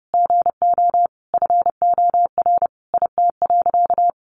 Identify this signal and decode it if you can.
Morse code, heard as a keyed tone, about 30 wpm, 700 Hz. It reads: GO FOR IT.
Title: Sentences from Top 1100 Words 30wpm